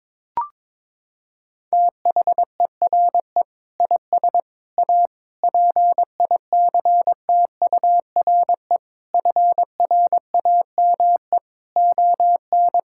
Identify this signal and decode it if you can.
Morse code, heard as a keyed tone, about 22 wpm, 700 Hz. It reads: THERE IS A PICTURE FRAME ON
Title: Sentences from Top 1000 Words - Review Part 6 - 22wpm